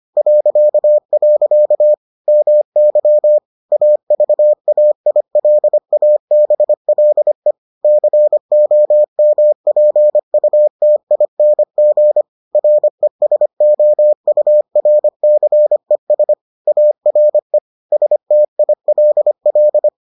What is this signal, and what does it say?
Morse code, heard as a keyed tone, about 25 wpm, 600 Hz. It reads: .. MY AVAILABLE COMPUTING RESOURCES ARE STILL